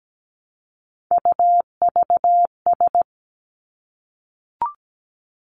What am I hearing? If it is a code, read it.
Morse code, heard as a keyed tone, about 17 wpm, 700 Hz. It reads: UVS
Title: Three Challenging Letter-Number Combinations 17wpm